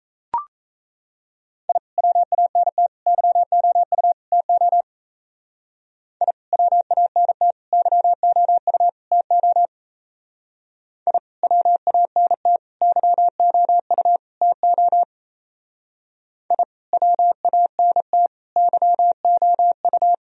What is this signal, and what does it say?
Morse code, heard as a keyed tone, about 35 wpm, 700 Hz. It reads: I WANT YOU TO I WANT YOU TO I WANT YOU TO I WANT YOU